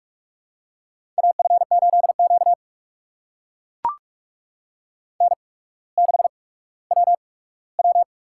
Morse code, heard as a keyed tone, about 41 wpm, spaced as for 19 wpm, 700 Hz. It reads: AF8Q N6WW
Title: USA Call Signs - 8x Character Spacing - 45wpm